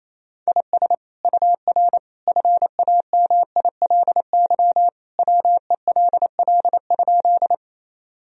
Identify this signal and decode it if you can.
Morse code, heard as a keyed tone, about 28 wpm, 700 Hz. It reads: IS UR FAMILY WELL?
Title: Sentences from Top 400 Words and Abbreviations 28wpm